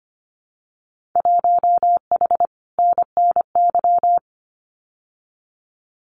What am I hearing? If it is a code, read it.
Morse code, heard as a keyed tone, about 25 wpm, 700 Hz. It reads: 1H NNY
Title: Winter Field Day Exchanges 25wpm